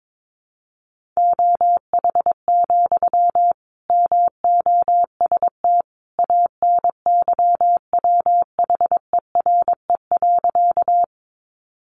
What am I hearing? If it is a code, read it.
Morse code, heard as a keyed tone, about 22 wpm, 700 Hz. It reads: OH, MOST ANYWHERE.
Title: Sentences from Top 1400 Words 22wpm